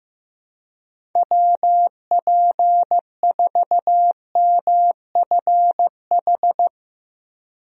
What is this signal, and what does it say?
Morse code, heard as a keyed tone, about 15 wpm, 700 Hz. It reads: WP4MFH
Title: USA Call Signs - 6 Characters - 15wpm